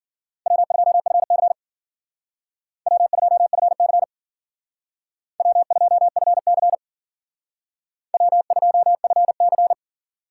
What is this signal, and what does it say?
Morse code, heard as a keyed tone, about 51 wpm, 700 Hz. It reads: W2FC W2FC W2FC W2FC